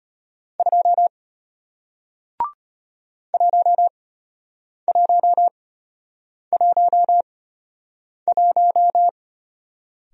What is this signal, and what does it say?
Morse code, heard as a keyed tone, about 33 wpm, spaced as for 8 wpm, 700 Hz. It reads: 2 1111